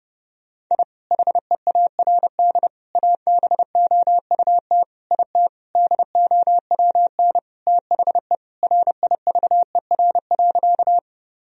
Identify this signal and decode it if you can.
Morse code, heard as a keyed tone, about 32 wpm, 700 Hz. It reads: I HEARD ABOUT IT DOWN THE RIVER.
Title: Sentences from Top 1000 Words - Review Part 1 - 30wpm